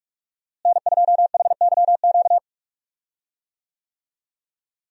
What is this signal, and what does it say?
Morse code, heard as a keyed tone, about 41 wpm, 700 Hz. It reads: N2HYQ
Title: USA Call Signs - 5 Characters - Repeated 2x - 45wpm